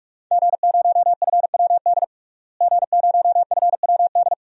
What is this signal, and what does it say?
Morse code, heard as a keyed tone, about 43 wpm, 700 Hz. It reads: G0FWD G0FWD
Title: DXCC-100 Call Signs - 5 Characters - Repeated 2x - 45wpm